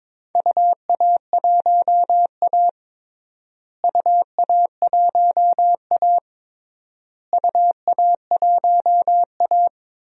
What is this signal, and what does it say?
Morse code, heard as a keyed tone, about 21 wpm, 700 Hz. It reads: UA1A UA1A UA1A
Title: DXCC-100 Call Signs - 4 Characters - Repeated 3x - 22wpm